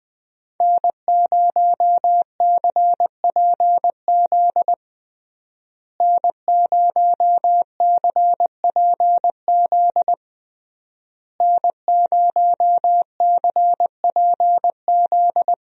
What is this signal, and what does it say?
Morse code, heard as a keyed tone, about 20 wpm, 700 Hz. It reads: N0CPZ N0CPZ N0CPZ
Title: USA Call Signs - 5 Characters - Repeated 3x - 20wpm